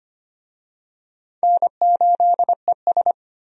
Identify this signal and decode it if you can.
Morse code, heard as a keyed tone, about 25 wpm, 700 Hz. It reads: N8ES
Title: USA Call Signs - 4 Characters - 25wpm